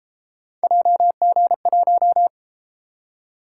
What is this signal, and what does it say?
Morse code, heard as a keyed tone, about 33 wpm, 700 Hz. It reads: JG1